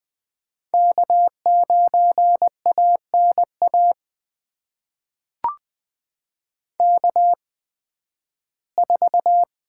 Morse code, heard as a keyed tone, about 20 wpm, spaced as for 13 wpm, 700 Hz. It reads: K9ANA K4
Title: USA Call Signs - 5 Characters - 8x Character Spacing - 20wpm